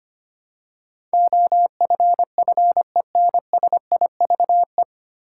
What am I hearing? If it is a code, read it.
Morse code, heard as a keyed tone, about 25 wpm, 700 Hz. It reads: OFFENSIVE
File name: Nine Letter Words - 25wpm